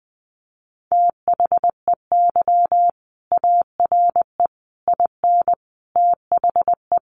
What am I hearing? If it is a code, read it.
Morse code, heard as a keyed tone, about 20 wpm, 700 Hz. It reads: THEY ARE IN THE
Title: Sentences from Top 600 Words 20wpm